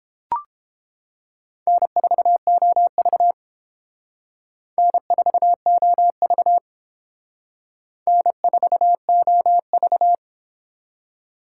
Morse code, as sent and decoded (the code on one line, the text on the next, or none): -. ....- --- ...- / -. ....- --- ...- / -. ....- --- ...-
N4OV N4OV N4OV